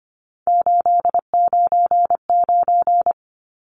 Morse code, heard as a keyed tone, about 25 wpm, 700 Hz. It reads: 899
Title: Numbers - 3-Digit 25wpm